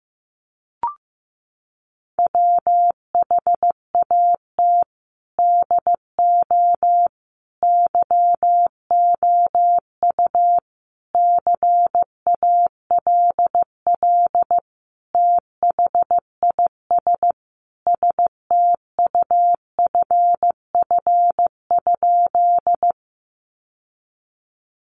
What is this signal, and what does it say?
Morse code, heard as a keyed tone, about 15 wpm, 700 Hz. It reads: WHAT DO YOU CALL THIS STUFF?
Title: Sentences from Top 1400 Words 15wpm